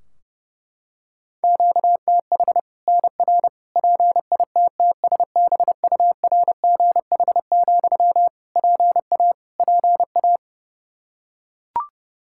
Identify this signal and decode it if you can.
Morse code, heard as a keyed tone, about 30 wpm, 700 Hz. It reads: QTH NR PITTSBURGH, PA PA
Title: Practice QSO - Exchange 1 - Beginning - 30wpm